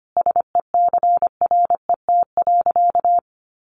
Morse code, heard as a keyed tone, about 25 wpm, 700 Hz. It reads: SECRET.
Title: Sentences from Top 1300 Words 25wpm